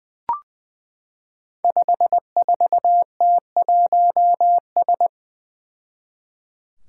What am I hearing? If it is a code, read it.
Morse code, heard as a keyed tone, about 20 wpm, 700 Hz. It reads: 54T1S
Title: Five Challenging Letter-Number Combinations 20wpm